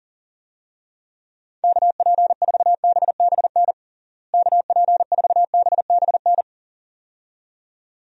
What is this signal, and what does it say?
Morse code, heard as a keyed tone, about 40 wpm, 700 Hz. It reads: KP4BBN KP4BBN
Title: DXCC-100 Call Signs - 6 Characters - Repeated 2x - 40wpm